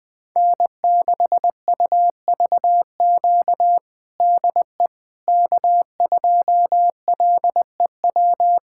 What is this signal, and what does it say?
Morse code, heard as a keyed tone, about 21 wpm, 700 Hz. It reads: N6UVQ DE K2LEW
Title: USA Callsign DE Callsign - 5 Characters - 20wpm